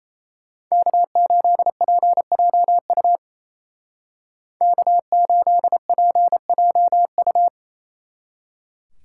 Morse code, heard as a keyed tone, about 33 wpm, 700 Hz. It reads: K8PJU K8PJU
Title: Speed Racing - USA 1x3 Call Signs 28wpm